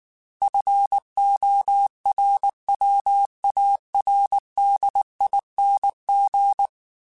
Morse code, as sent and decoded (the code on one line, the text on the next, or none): ..-. --- .-. .-- .- .-. -.. .. -. --.
FORWARDING